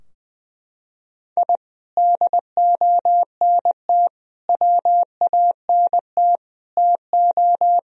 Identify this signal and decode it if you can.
Morse code, heard as a keyed tone, about 20 wpm, 700 Hz. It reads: I DONT WANT TO
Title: Sentences from Top 1000 Words - Review Part 4 - 20wpm